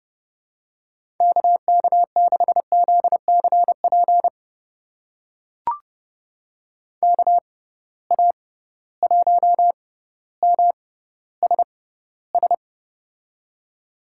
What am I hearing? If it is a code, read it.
Morse code, heard as a keyed tone, about 30 wpm, spaced as for 15 wpm, 700 Hz. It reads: KK6ZCP KA1MSS
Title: USA Call Signs - 6 Characters - 6x Character Spacing - 30wpm